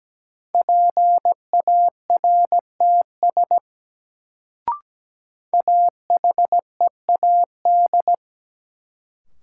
Morse code, heard as a keyed tone, about 17 wpm, 700 Hz. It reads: PARTS AHEAD
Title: Sentences from Top 1000 Words - Review Part 6 - New Word Drill - 17wpm